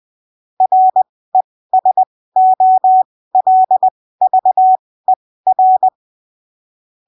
Morse code, heard as a keyed tone, about 20 wpm, spaced as for 15 wpm, 750 Hz. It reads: RESOLVER
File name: 20ppm